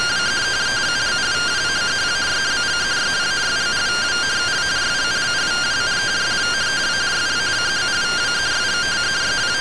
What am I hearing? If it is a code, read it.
Two-tone teleprinter signal, about 170 Hz shift.